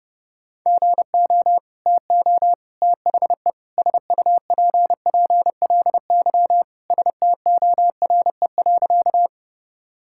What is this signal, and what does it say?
Morse code, heard as a keyed tone, about 31 wpm, 700 Hz. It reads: GO TO THE SUPPLY STORE.
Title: Sentences from Top 1000 Words - Review Part 5 - 30wpm